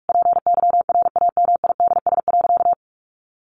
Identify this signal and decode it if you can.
Morse code, heard as a keyed tone, about 44 wpm, 700 Hz. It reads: PYRAMIDS.
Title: A Sound Of Thunder 45wpm